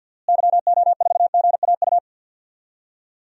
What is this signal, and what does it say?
Morse code, heard as a keyed tone, about 51 wpm, 700 Hz. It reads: YO4GAU
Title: DXCC-100 Call Signs - 6 Characters - Repeated 3x - 50wpm